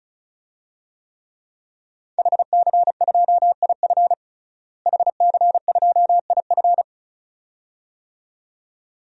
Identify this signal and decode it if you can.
Morse code, heard as a keyed tone, about 36 wpm, 700 Hz. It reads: HC2IF HC2IF
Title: DXCC-100 Call Signs - 5 Characters - Repeated 2x - 35wpm